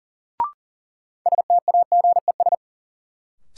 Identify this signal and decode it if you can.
Morse code, heard as a keyed tone, about 40 wpm, 700 Hz. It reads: STAGES
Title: Sets of 5 Words Encore - New Word Drill - 40wpm